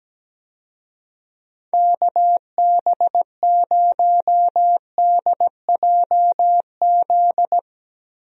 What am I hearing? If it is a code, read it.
Morse code, heard as a keyed tone, about 17 wpm, 700 Hz. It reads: KB0DJZ
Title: USA Call Signs - 6 Characters - 17wpm